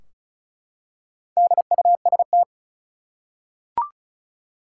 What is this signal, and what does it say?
Morse code, heard as a keyed tone, about 36 wpm, 700 Hz. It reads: DUST
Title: Four Letter Words - 35wpm